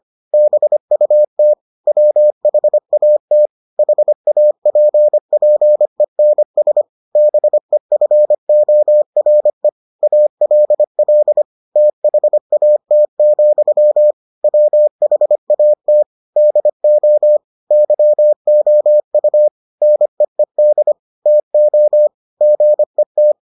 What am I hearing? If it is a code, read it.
Morse code, heard as a keyed tone, about 25 wpm, 600 Hz. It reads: BUT WHAT HAPPENS BEFORE ALL THAT, WHAT DO YOU NEED TO GET